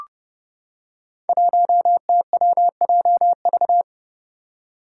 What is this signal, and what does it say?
Morse code, heard as a keyed tone, about 29 wpm, 700 Hz. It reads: 1TWJV
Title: Five Challenging Letter-Number Combinations 30wpm